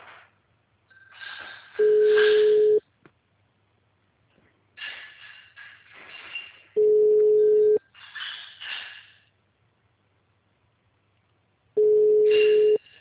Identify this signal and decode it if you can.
Ringback tone (Europe, 425 Hz).